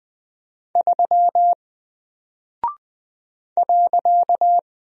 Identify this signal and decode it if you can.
Morse code, heard as a keyed tone, about 20 wpm, 700 Hz. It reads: 3 .
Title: Single Character (Letter:Number:Punctuation:Prosign) 20wpm